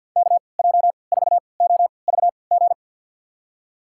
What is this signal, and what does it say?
Morse code, heard as a keyed tone, about 53 wpm, spaced as for 29 wpm, 700 Hz. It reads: KJ4OVG
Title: USA Call Signs - 6 Characters - 3x Character Spacing - 50wpm